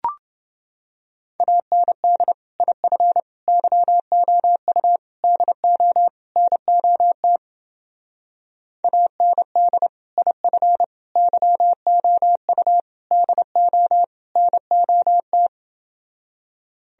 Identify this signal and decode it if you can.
Morse code, heard as a keyed tone, about 29 wpm, 700 Hz. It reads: AND IF YOU DO NOT AND IF YOU DO NOT